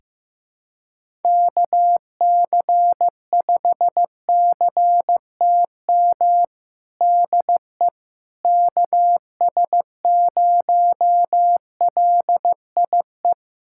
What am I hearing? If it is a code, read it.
Morse code, heard as a keyed tone, about 16 wpm, 700 Hz. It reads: KC5CTM DE KS0LIE
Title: USA Callsign DE Callsign - 6 Characters - 15wpm